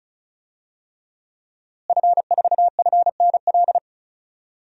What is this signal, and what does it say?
Morse code, heard as a keyed tone, about 36 wpm, 700 Hz. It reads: F4FNL
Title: DXCC-100 Call Signs - 5 Characters - Repeated 3x - 35wpm